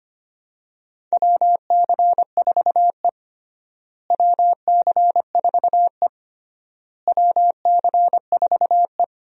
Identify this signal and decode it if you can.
Morse code, heard as a keyed tone, about 25 wpm, 700 Hz. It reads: WC4E WC4E WC4E
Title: USA Call Signs - 4 Characters - Repeated 3x - 25wpm